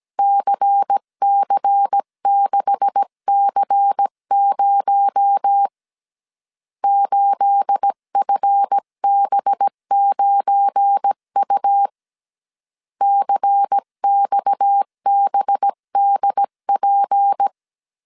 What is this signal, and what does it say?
Morse code, heard as a keyed tone, about 17 wpm, 800 Hz. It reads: CC6C0 8FB9U CXBDP
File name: CW / DET / 1 9 1500C S B 16wpm 2856